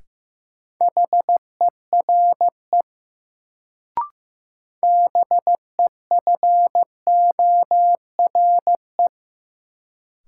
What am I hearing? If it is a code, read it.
Morse code, heard as a keyed tone, about 15 wpm, 700 Hz. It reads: HERE BEFORE
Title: Sentences from Top 200 Words - New Word Drill - 15wpm